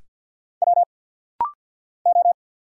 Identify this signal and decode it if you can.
Morse code, heard as a keyed tone, about 50 wpm, 700 Hz. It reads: W O
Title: Single Letter-Number Alt - Rapid-Fire 50wpm